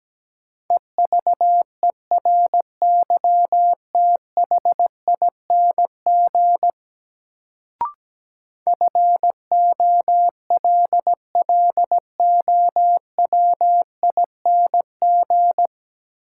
Morse code, heard as a keyed tone, about 17 wpm, 700 Hz. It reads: EVERYTHING FOLLOWING
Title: Sentences from Top 700 Words - New Word Drill - 17wpm